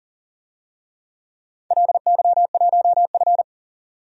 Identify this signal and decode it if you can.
Morse code, heard as a keyed tone, about 44 wpm, 690 Hz. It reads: LY1F
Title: DXCC-100 Call Signs - 4 Characters - 40wpm